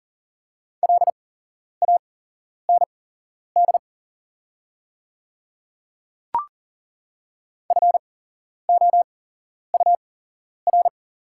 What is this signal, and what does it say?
Morse code, heard as a keyed tone, about 40 wpm, spaced as for 11 wpm, 690 Hz. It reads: LAND FOUR
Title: 8x Character Spacing - Top 500 Words - 40wpm